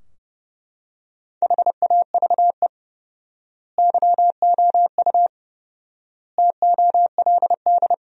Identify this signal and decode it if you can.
Morse code, heard as a keyed tone, about 30 wpm, 700 Hz. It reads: HAVE YOU TOLD
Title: 4x Word-Spacing - Sentences from Top 500 Words - 30wpm